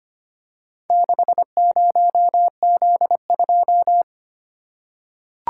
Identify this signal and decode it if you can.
Morse code, heard as a keyed tone, about 25 wpm, 700 Hz. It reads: 60Z2
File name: practice / 25wpm / Four Letter-Number Combinations - 25wpm